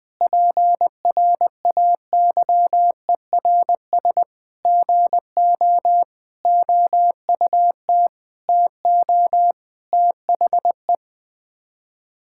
Morse code, heard as a keyed tone, about 20 wpm, 700 Hz. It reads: PRAYERS GO OUT TO THE
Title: Sets of 8 Words - 20wpm